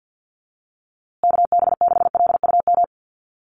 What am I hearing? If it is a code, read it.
Morse code, heard as a keyed tone, about 50 wpm, 700 Hz. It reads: KB6LUM
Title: USA Call Signs Repeated 2x 50wpm